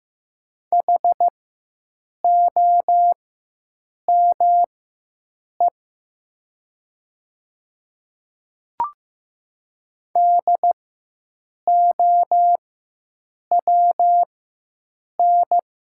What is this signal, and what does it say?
Morse code, heard as a keyed tone, about 15 wpm, spaced as for 7 wpm, 700 Hz. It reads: HOME DOWN
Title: 4x Character Spacing - Top 200 Words - 15wpm